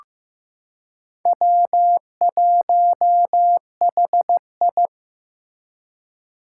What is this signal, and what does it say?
Morse code, heard as a keyed tone, about 15 wpm, 700 Hz. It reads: W1HI
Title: Four Challenging Letter-Number Combinations 15wpm